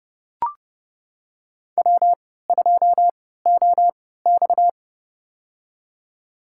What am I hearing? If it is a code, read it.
Morse code, heard as a keyed tone, about 32 wpm, spaced as for 17 wpm, 700 Hz. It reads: W2OX
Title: USA Call Signs - 3x Character Spacing - 30wpm